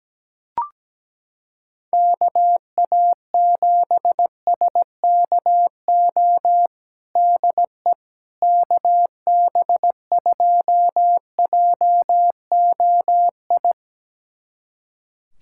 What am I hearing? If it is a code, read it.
Morse code, heard as a keyed tone, about 17 wpm, 700 Hz. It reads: KA7SKO DE KB2JOI
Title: USA Callsign DE Callsign - 6 Characters - 17wpm